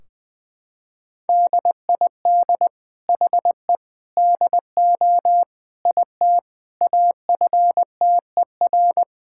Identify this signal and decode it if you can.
Morse code, heard as a keyed tone, about 20 wpm, 700 Hz. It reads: DID HE DO IT AFTER
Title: Sentences from Top 1600 Words 20wpm